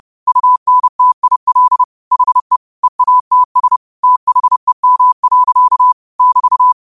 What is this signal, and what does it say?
Morse code, heard as a keyed tone, about 30 wpm, 1000 Hz. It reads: UNTIL HE EATS THEM. <BT>
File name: qotdfiles / 30wpm